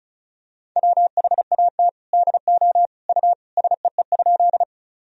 Morse code, heard as a keyed tone, about 36 wpm, 700 Hz. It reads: WHAT DO U SEE?
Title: Sentences from Top 100 Words and Abbreviations 35wpm